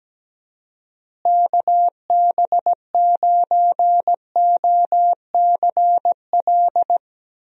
Morse code, heard as a keyed tone, about 17 wpm, 700 Hz. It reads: KB9OCL
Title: USA Call Signs - 6 Characters - Repeated 3x - 17wpm